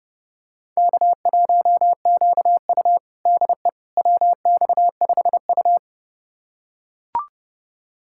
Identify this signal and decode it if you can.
Morse code, heard as a keyed tone, about 30 wpm, 700 Hz. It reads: K1QU DE WX5U
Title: USA Callsign DE Callsign - 4 Characters - 30wpm